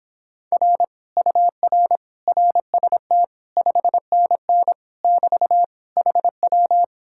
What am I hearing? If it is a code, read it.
Morse code, heard as a keyed tone, about 31 wpm, 700 Hz. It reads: R UR RST 5NN <BT> HW